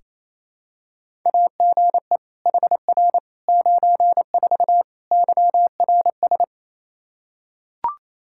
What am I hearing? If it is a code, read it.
Morse code, heard as a keyed tone, about 28 wpm, 700 Hz. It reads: AGE HR 94 YRS
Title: Practice QSO - Exchange 3 - More Info - 28wpm